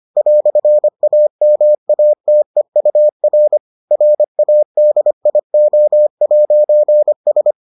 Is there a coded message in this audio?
Yes: Morse code from 0.2 to 7.5 s, heard as a keyed tone, about 25 wpm, 600 Hz. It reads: "AMATEUR RADIO'S